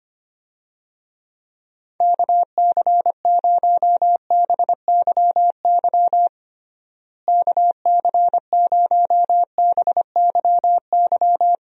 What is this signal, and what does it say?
Morse code, heard as a keyed tone, about 25 wpm, 700 Hz. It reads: KC0BYY KC0BYY
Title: USA Call Signs - 6 Characters - Repeated 2x - 25wpm